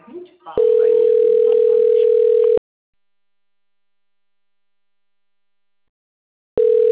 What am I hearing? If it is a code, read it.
Ringback tone (North America), 440 + 480 Hz, with speech.
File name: rg-1600-unknown- - - .36067